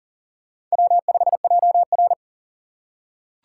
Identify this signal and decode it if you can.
Morse code, heard as a keyed tone, about 40 wpm, 700 Hz. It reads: W5JR
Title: USA Call Signs - 4 Characters - 3x Character Spacing - 40wpm